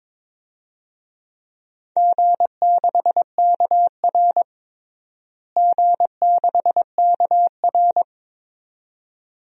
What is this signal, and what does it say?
Morse code, heard as a keyed tone, about 21 wpm, 700 Hz. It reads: G6KR G6KR
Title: DXCC-100 Call Signs - 4 Characters - Repeated 2x - 22wpm